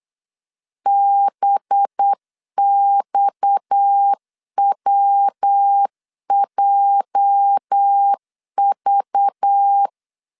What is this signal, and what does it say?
Morse code, heard as a keyed tone, about 8 wpm, 800 Hz. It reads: BXWJV
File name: LEZIONI / 8 YJ45 / 8 YJ 8wpm 3313